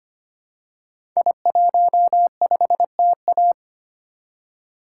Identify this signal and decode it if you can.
Morse code, heard as a keyed tone, about 25 wpm, 690 Hz. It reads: I15TA
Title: Code Groups - 5 Letter:Numbers - 25wpm